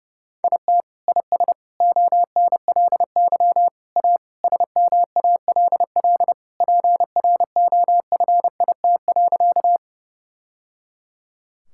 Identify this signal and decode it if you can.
Morse code, heard as a keyed tone, about 32 wpm, 700 Hz. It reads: IT IS ONLY A SMALL PROFIT.
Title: Sentences from Top 1000 Words - Review Part 5 - 30wpm